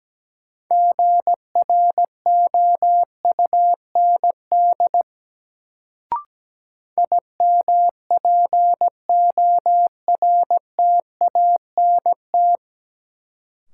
Morse code, heard as a keyed tone, about 17 wpm, 700 Hz. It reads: GROUND IMPORTANT
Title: Sets of 3 Words - New Word Drill - 17wpm